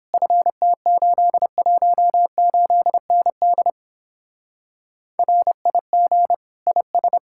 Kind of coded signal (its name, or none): Morse code